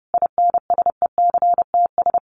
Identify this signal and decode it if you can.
Morse code, heard as a keyed tone, about 30 wpm, 700 Hz. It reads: INSECTS
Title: A Sound Of Thunder 30wpm